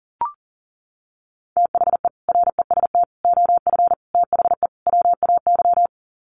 Morse code, heard as a keyed tone, about 40 wpm, 700 Hz. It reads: THE REST OF THE WAY
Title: Sets of 5 Words Encore 40wpm